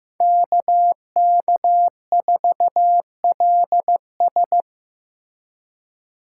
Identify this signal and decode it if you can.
Morse code, heard as a keyed tone, about 15 wpm, 700 Hz. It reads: KK4LS
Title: USA Call Signs - 5 Characters - 2x Character Spacing - 15wpm